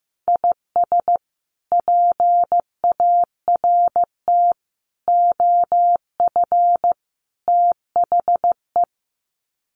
Morse code, heard as a keyed tone, about 15 wpm, 700 Hz. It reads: IS PART OF THE
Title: Sets of 4 Words 15wpm